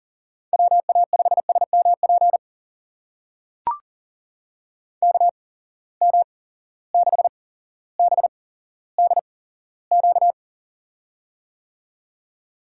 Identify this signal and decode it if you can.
Morse code, heard as a keyed tone, about 40 wpm, spaced as for 17 wpm, 700 Hz. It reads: WA5SMP KM6BDQ
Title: USA Call Signs - 6 Characters - 8x Character Spacing - 40wpm